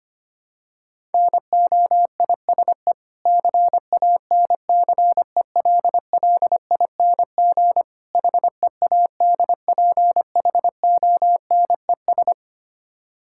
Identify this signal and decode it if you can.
Morse code, heard as a keyed tone, about 25 wpm, 700 Hz. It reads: NOISE CANCELLING HEADPHONES
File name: practice / 25wpm / Difficult Words with Hints - 25wpm